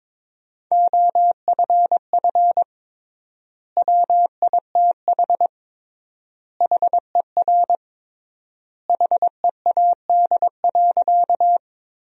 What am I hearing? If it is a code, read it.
Morse code, heard as a keyed tone, about 22 wpm, 700 Hz. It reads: OFF WITH HER HEAD.
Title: 3x Word-Spacing - Sentences from Top 300 Words - 22wpm